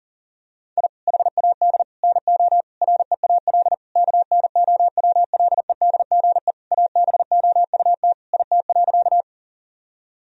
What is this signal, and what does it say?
Morse code, heard as a keyed tone, about 40 wpm, 700 Hz. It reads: I HAD NO REAL KNOWLEDGE ABOUT IT.
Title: Sentences from Top 1000 Words - Review Part 3 - 40wpm